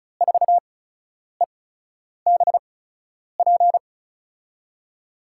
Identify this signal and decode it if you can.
Morse code, heard as a keyed tone, about 36 wpm, spaced as for 10 wpm, 700 Hz. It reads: 4EBP
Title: USA Call Signs - 8x Character Spacing - 35wpm